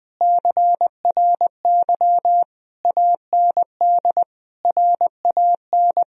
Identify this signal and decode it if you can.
Morse code, heard as a keyed tone, about 21 wpm, 700 Hz. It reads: CRY AND RAN
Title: Sentences from Top 1000 Words - Review Part 3 - 20wpm